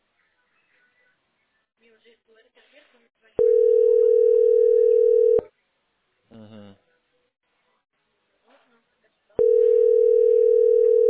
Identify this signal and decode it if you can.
Ringback tone (North America), 440 + 480 Hz, with speech.
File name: 8d3e-4af1-a3c5